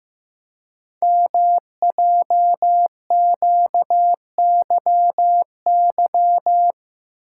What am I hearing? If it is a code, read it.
Morse code, heard as a keyed tone, about 15 wpm, 700 Hz. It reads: MJQYY
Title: Code Groups - 5 Letters - 15wpm